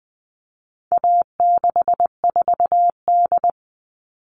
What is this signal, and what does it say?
Morse code, heard as a keyed tone, about 20 wpm, 700 Hz. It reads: A64D
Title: All Cut Numbers - 4-Digit 20wpm